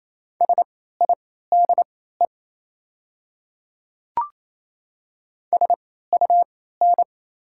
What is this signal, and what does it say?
Morse code, heard as a keyed tone, about 27 wpm, spaced as for 16 wpm, 700 Hz. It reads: SIDE SUN
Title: 3x Character Spacing - Top 300 Words - 28wpm